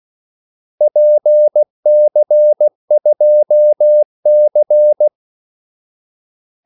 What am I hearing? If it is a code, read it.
Morse code, heard as a keyed tone, about 16 wpm, 600 Hz. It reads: PC2C
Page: Kaldesignaler 16 wpm